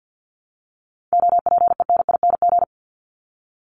Morse code, heard as a keyed tone, about 50 wpm, 700 Hz. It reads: OPENING
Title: ING Suffix - Top 100 Words with Base Words 50wpm